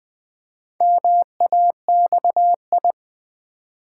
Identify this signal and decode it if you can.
Morse code, heard as a keyed tone, about 20 wpm, 700 Hz. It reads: MAXI
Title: Build A Word - 2 Letters At A Time - 8-Letter Words - 20wpm